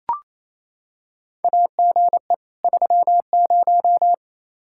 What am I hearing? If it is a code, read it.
Morse code, heard as a keyed tone, about 28 wpm, 700 Hz. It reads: AGE 30
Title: Practice QSO - Exchange 3 - More Info - 28wpm